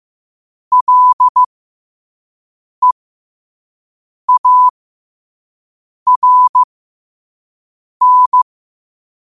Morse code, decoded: LEARN

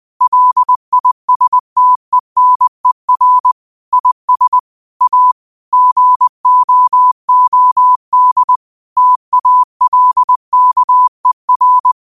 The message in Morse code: LISTENER IS A GOOD TALKER

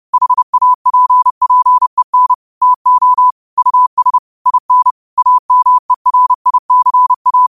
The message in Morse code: HAPPEN TO US IN AMERICA

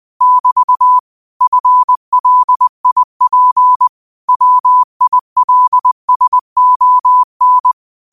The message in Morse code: <BT> FLIP WILSON

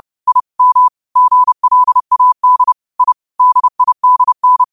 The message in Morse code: I M GLAD I DIDN

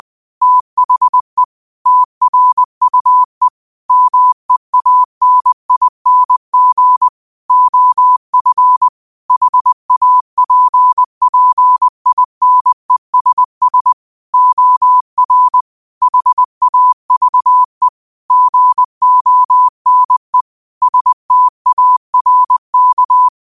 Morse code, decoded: THE TRUE MEANING OF HAPPINESS OR HAVE GONE STARK

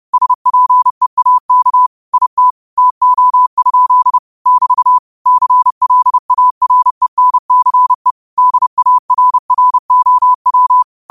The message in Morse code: SPEAK IT TO? <BT> CLARENCE DARROW

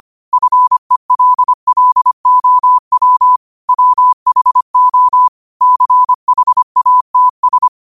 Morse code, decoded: FELLOW WHO CHATS

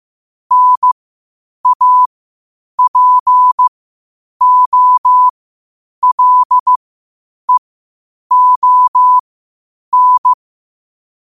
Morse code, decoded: NAPOLEON

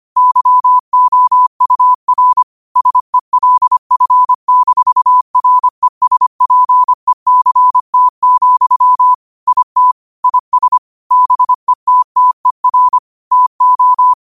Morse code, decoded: YOUR SELF-RESPECT, IT IS BETTER TO

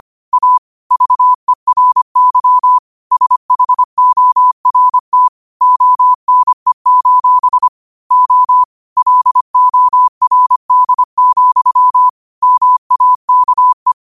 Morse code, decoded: A VERY SHORT ONE: O LORD, MAKE

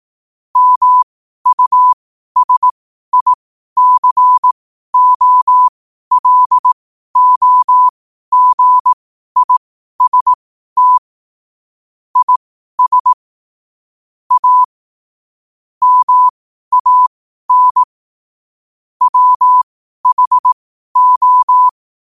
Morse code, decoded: MUSICOLOGIST IS A MAN WHO

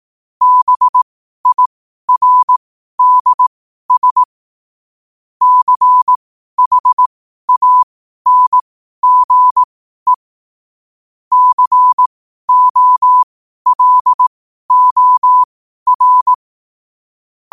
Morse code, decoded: BIRDS CHANGE COLOR

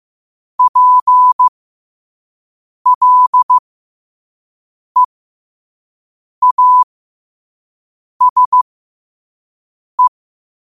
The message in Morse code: PLEASE

